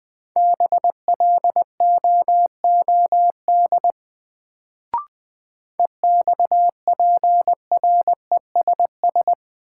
Morse code, heard as a keyed tone, about 20 wpm, 700 Hz. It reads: BLOOD EXPRESS
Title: Sentences from Top 1100 Words - New Word Drill Alt - 20wpm